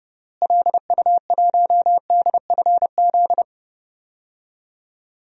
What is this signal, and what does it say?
Morse code, heard as a keyed tone, about 30 wpm, 700 Hz. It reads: LU1DFZ
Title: DXCC-100 Call Signs - 6 Characters - Repeated 2x - 30wpm